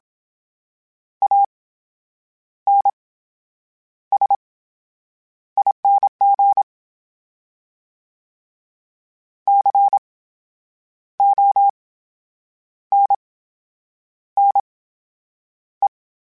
Morse code, heard as a keyed tone, about 27 wpm, spaced as for 8 wpm, 800 Hz. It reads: ANSING CONNE
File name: Special INGs / Special INGs-mp3s / VerySlowToFaster Ing @25